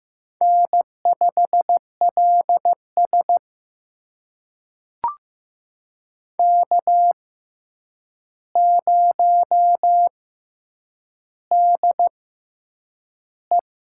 Morse code, heard as a keyed tone, about 15 wpm, spaced as for 8 wpm, 700 Hz. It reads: N5LS K0DE
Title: USA Call Signs - 6x Character Spacing - 15wpm